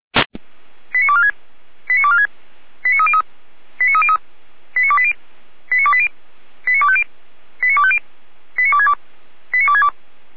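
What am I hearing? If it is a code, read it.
Five-tone selective call (ZVEI), heard as quick runs of five tones, tones of about 70 ms. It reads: 89246 89246 89303 89303 89280 89280 89250 89250 89262 89262
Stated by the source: ZVEI five-tone call 89174 received by SDR.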